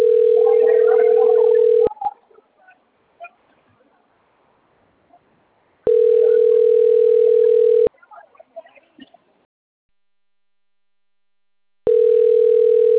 Ringback tone (North America), 440 + 480 Hz.
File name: rg-1600-unknown- - - .4053